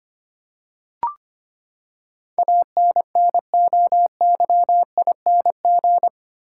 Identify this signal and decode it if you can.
Morse code, heard as a keyed tone, about 25 wpm, 700 Hz. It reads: ANNOYING
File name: Eight Letter Words - 25wpm